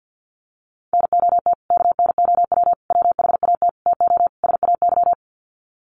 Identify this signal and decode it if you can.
Morse code, heard as a keyed tone, about 50 wpm, 700 Hz. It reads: NOT KNOW WHAT TO SAY
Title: Sets of 5 Words 50wpm